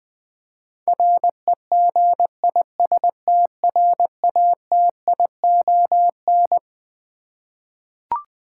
Morse code, heard as a keyed tone, about 20 wpm, 700 Hz. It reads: REGISTRATION
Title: Twelve Letter Words - 20wpm